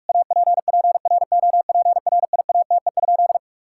Morse code, heard as a keyed tone, about 41 wpm, 700 Hz. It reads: APPROPRIATE?
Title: Sentences from Top 1000 Words - Review Part 5 - 45wpm